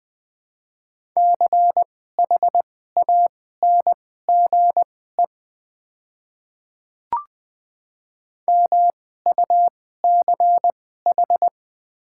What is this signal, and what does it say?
Morse code, heard as a keyed tone, about 20 wpm, spaced as for 14 wpm, 700 Hz. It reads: CHANGE MUCH
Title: 2x Character Spacing - Top 200 Words - 20wpm